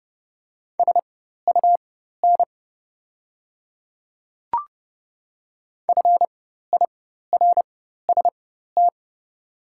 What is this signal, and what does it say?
Morse code, heard as a keyed tone, about 30 wpm, spaced as for 14 wpm, 700 Hz. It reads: SUN FIRST
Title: 4x Character Spacing - Top 300 Words - 30wpm